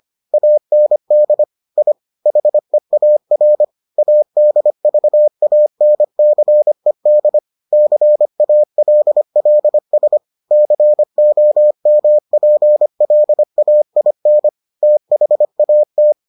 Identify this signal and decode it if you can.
Morse code, heard as a keyed tone, about 25 wpm, 600 Hz. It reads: AND I HEAR ADVANCED CALLS COMPLAIN THAT